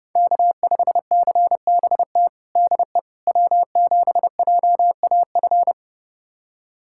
Morse code, heard as a keyed tone, about 30 wpm, 700 Hz. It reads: K5CBT DE W7JAF
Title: USA Callsign DE Callsign - 5 Characters - 30wpm